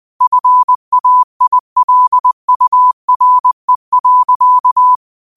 Morse code, heard as a keyed tone, about 20 wpm, 1000 Hz. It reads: FAILURE.
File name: qotdfiles / 20wpm